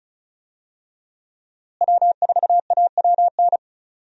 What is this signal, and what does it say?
Morse code, heard as a keyed tone, about 34 wpm, 700 Hz. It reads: W4AWN
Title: USA Call Signs - 5 Characters - Repeated 3x - 35wpm